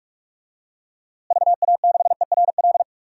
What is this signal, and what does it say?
Morse code, heard as a keyed tone, about 46 wpm, 700 Hz. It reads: VA6ERL
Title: DXCC-100 Call Signs - 6 Characters - Repeated 2x - 45wpm